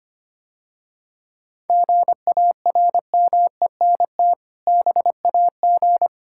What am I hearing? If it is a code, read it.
Morse code, heard as a keyed tone, about 25 wpm, 690 Hz. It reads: GARMENT BAG
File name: practice / 25wpm / Difficult Words - Speed-Racing with Hints - 25wpm